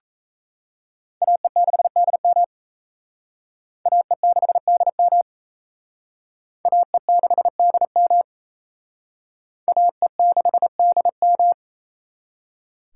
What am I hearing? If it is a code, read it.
Morse code, heard as a keyed tone, about 35 wpm, 700 Hz. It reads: AE6DM AE6DM AE6DM AE6DM
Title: Speed Racing - USA 2x2 Call Signs 28wpm